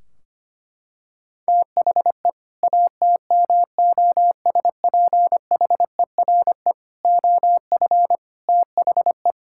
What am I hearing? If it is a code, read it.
Morse code, heard as a keyed tone, about 25 wpm, 700 Hz. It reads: THE ATMOSPHERE OF THE
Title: Sentences from Top 1600 Words 25wpm